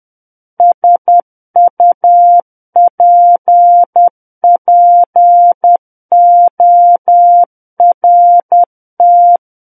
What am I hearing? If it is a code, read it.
Morse code, heard as a keyed tone, about 10 wpm, 700 Hz. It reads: SUPPORT
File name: Book One / 01-Chapter 1 - 10 WPM